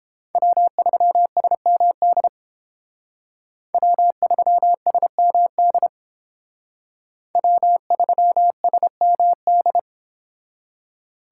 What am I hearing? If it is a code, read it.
Morse code, heard as a keyed tone, about 31 wpm, 700 Hz. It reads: W3SMD W3SMD W3SMD